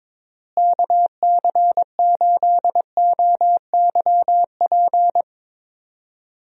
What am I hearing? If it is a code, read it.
Morse code, heard as a keyed tone, about 22 wpm, 700 Hz. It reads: KC8OYP
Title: USA Call Signs - 6 Characters - Repeated 2x - 22wpm